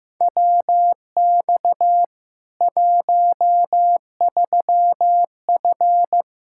Morse code, heard as a keyed tone, about 15 wpm, 700 Hz. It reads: WX 13F
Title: K1USN SST QSO Practice 15wpm